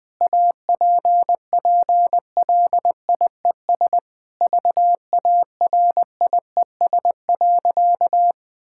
Morse code, heard as a keyed tone, about 20 wpm, 700 Hz. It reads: APPLIES VARIES.
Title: Sentences with 5-Word Phrases - 20wpm